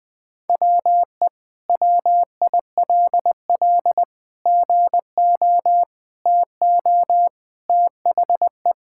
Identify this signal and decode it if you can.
Morse code, heard as a keyed tone, about 20 wpm, 700 Hz. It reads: WE WILL GO TO THE
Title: Sentences with 2-Word Phrases - 20wpm